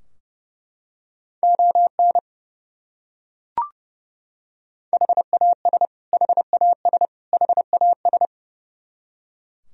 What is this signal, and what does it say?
Morse code, heard as a keyed tone, about 30 wpm, 700 Hz. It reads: ON HAS HAS HAS
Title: Top 100 Words Repeated 30wpm